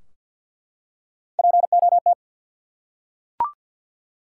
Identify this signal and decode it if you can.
Morse code, heard as a keyed tone, about 49 wpm, 700 Hz. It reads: POT
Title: Three Letter Words - 50wpm